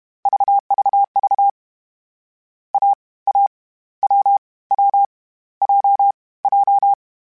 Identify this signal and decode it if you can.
Morse code, heard as a keyed tone, about 32 wpm, 800 Hz. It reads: VVV A A W W J J
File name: Difficult Combos / Difficult Combos-mp3s / awj1 ndb6 gz7 @30